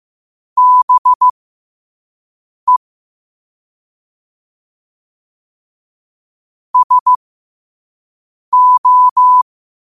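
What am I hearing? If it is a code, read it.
Morse code, heard as a keyed tone, about 15 wpm, spaced as for 5 wpm, 1000 Hz. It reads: BE SO
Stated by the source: Morse sent at 5 wpm.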